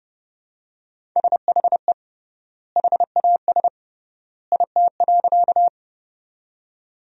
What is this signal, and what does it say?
Morse code, heard as a keyed tone, about 30 wpm, 700 Hz. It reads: SHE HAS IT.
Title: 3x Word-Spacing - Sentences from Top 100 Words - 30wpm